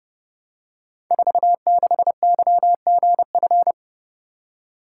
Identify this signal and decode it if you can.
Morse code, heard as a keyed tone, about 29 wpm, 700 Hz. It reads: 46YGF
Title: Code Groups - 5 Letter:Numbers - 30wpm